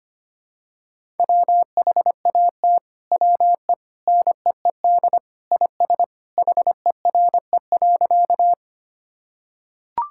Morse code, heard as a keyed tone, about 25 wpm, 700 Hz. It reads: WHAT WE NEED IS HERE.
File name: practice / 25wpm / Sentences with 2-Word Phrases - 25wpm